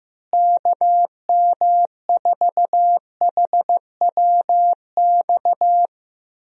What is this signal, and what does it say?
Morse code, heard as a keyed tone, about 15 wpm, 700 Hz. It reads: KM4HWX
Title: USA Call Signs - 6 Characters - Repeated 2x - 15wpm